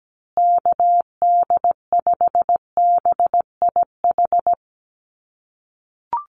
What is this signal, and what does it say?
Morse code, heard as a keyed tone, about 17 wpm, 700 Hz. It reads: KD5BIH
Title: USA Call Signs Repeated 3x 17wpm